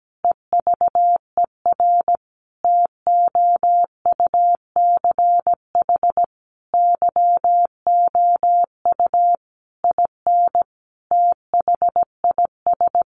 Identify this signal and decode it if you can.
Morse code, heard as a keyed tone, about 17 wpm, 700 Hz. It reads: EVER TOUCH YOU IN THIS
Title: Sentences from Top 1200 Words 17wpm